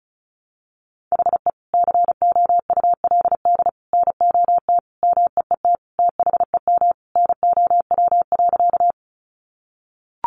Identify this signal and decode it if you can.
Morse code, heard as a keyed tone, about 34 wpm, 700 Hz. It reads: HE COULD NOT MEET THEM NOW.
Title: Sentences from Top 1100 Words 35wpm